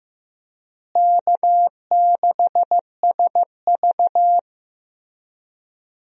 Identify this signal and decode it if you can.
Morse code, heard as a keyed tone, about 15 wpm, 700 Hz. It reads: K6SV
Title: USA Call Signs - 4 Characters - Repeated 2x - 15wpm